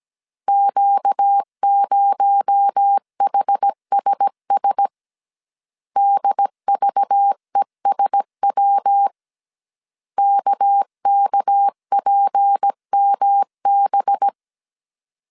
Morse code, heard as a keyed tone, about 17 wpm, 800 Hz. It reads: Q0HSS DVESW KKPMB
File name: CW / DET / 1 6 1500C S B 16wpm 2619